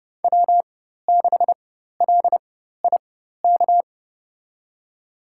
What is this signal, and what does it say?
Morse code, heard as a keyed tone, about 32 wpm, spaced as for 14 wpm, 700 Hz. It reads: W6LIK
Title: USA Call Signs - 4x Character Spacing - 30wpm